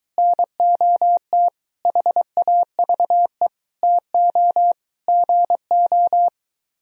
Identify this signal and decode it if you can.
Morse code, heard as a keyed tone, about 23 wpm, 700 Hz. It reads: NOT HAVE TO GO